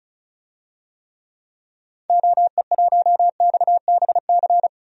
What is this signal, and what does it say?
Morse code, heard as a keyed tone, about 34 wpm, 700 Hz. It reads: OE1XBC
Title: DXCC-100 Call Signs - 6 Characters - 35wpm